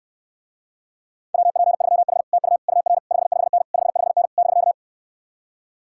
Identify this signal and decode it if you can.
Morse code, heard as a keyed tone, about 69 wpm, 700 Hz. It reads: WP3H TU UR 55N 55N <BK>